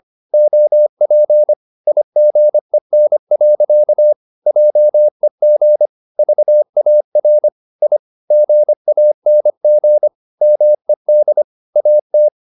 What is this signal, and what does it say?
Morse code, heard as a keyed tone, about 25 wpm, 600 Hz. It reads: OP IGEN. JEG VAR I GANG MED AT
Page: Ferie 25 wpm